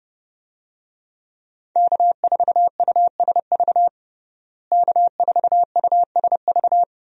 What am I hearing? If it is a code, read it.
Morse code, heard as a keyed tone, about 30 wpm, 700 Hz. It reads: K4USV K4USV
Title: USA Call Signs - 5 Characters - Repeated 2x - 30wpm